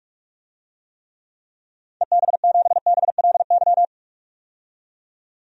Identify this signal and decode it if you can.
Morse code, heard as a keyed tone, about 41 wpm, 700 Hz. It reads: EB7BLY
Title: DXCC-100 Call Signs - 6 Characters - 45wpm